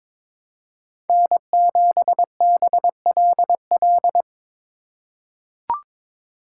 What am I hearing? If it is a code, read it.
Morse code, heard as a keyed tone, about 22 wpm, 700 Hz. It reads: N7BLL
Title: USA Call Signs - 5 Characters - 4x Character Spacing - 22wpm